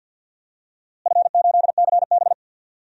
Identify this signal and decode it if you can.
Morse code, heard as a keyed tone, about 52 wpm, 700 Hz. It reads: F8CB
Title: DXCC-100 Call Signs - 4 Characters - Repeated 2x - 50wpm